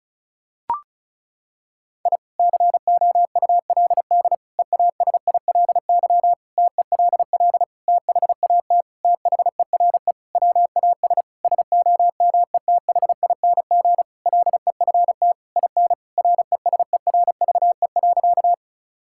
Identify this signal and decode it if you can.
Morse code, heard as a keyed tone, about 36 wpm, 700 Hz. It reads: I COULD EASILY TELL THAT THERE WAS SOMETHING LEFT IN RESERVE.
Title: Sentences from Top 1000 Words - Review Part 6 - 35wpm